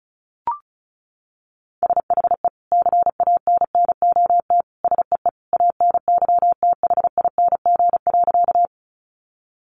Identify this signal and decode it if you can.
Morse code, heard as a keyed tone, about 34 wpm, 700 Hz. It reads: SHE CANNOT SEE ANYTHING.
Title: Additional Easy Phrases 35wpm